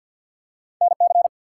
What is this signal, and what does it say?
Morse code, heard as a keyed tone, about 48 wpm, 700 Hz. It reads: NC
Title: US States and Canadian Provinces 50wpm